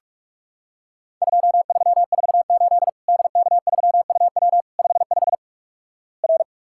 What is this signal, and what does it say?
Morse code, heard as a keyed tone, about 46 wpm, 700 Hz. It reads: 2348 DK3UW 55 R
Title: Worked All Europe DX Contest - QTC Exchange - 45wpm